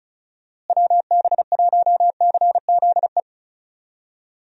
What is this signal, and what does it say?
Morse code, heard as a keyed tone, about 34 wpm, 700 Hz. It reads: WB1CZE
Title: USA Call Signs - 6 Characters - Repeated 3x - 35wpm